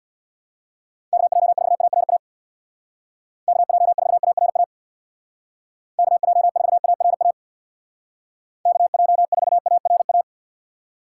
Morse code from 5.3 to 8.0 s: KJ4ARA

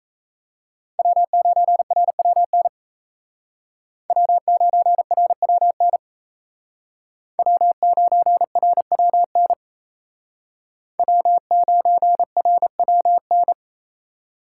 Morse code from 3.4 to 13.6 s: W9RWN W9RWN W9RWN

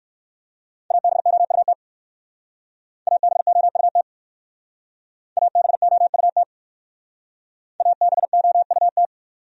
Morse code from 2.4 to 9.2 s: ABOUT ABOUT ABOUT

